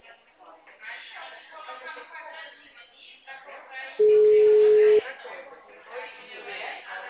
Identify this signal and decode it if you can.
Ringback tone (Europe, 425 Hz), with speech.
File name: out- -2007- - - .85797